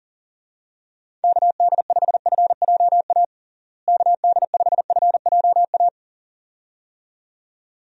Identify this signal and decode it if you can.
Morse code, heard as a keyed tone, about 37 wpm, 700 Hz. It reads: KD5FJA KD5FJA
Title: USA Call Signs - 6 Characters - Repeated 2x - 40wpm